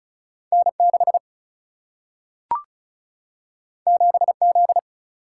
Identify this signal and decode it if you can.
Morse code, heard as a keyed tone, about 35 wpm, 700 Hz. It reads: N6 7Z
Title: Single-Letter Single-Number 35wpm